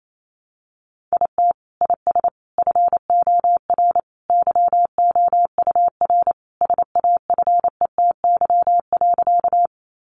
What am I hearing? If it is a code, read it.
Morse code, heard as a keyed tone, about 28 wpm, 700 Hz. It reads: IT IS FOR YOUR SAFETY.
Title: Sentences from Top 700 Words 28wpm